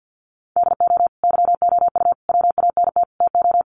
Morse code, heard as a keyed tone, about 48 wpm, 700 Hz. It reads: DO YOU WANT TO
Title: Sets of 4 Words 50wpm